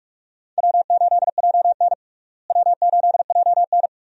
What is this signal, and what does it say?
Morse code, heard as a keyed tone, about 46 wpm, 700 Hz. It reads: W8JN W8JN
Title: USA Call Signs - 4 Characters - Repeated 2x - 45wpm